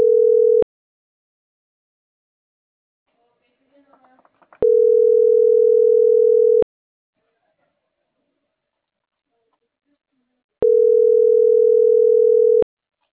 Ringback tone (North America), 440 + 480 Hz.
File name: CENTRAL MAYAB / out esmeralda.garcia